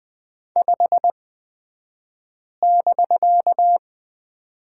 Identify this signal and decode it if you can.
Morse code, heard as a keyed tone, about 20 wpm, spaced as for 5 wpm, 700 Hz. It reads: 5<BK>
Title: 249 Lesson 40d - BK - Two Characters 20wpm